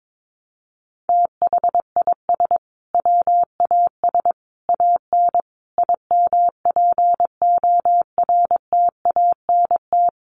Morse code, heard as a keyed tone, about 22 wpm, 700 Hz. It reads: THIS WAS AN IMPORTANT
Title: Sentences from Top 1300 Words 22wpm